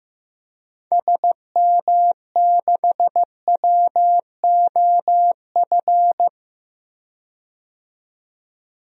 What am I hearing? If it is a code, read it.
Morse code, heard as a keyed tone, about 15 wpm, 700 Hz. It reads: SM6WOF
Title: DXCC-100 Call Signs - 6 Characters - 15wpm